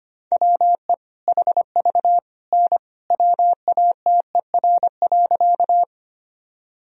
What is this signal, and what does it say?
Morse code, heard as a keyed tone, about 25 wpm, 700 Hz. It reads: WE HV N WATER.
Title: Sentences from Top 100 Words and Abbreviations 25wpm